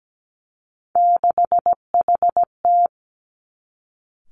Morse code, heard as a keyed tone, about 17 wpm, 700 Hz. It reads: 6HT
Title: Three Challenging Letter-Number Combinations 17wpm